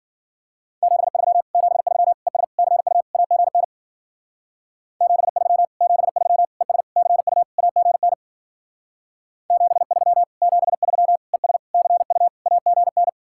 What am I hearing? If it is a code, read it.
Morse code, heard as a keyed tone, about 49 wpm, 700 Hz. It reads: 73 73 ES CU AGN 73 73 ES CU AGN 73 73 ES CU AGN
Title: Speed-Racing Common QSO Phrases - 40wpm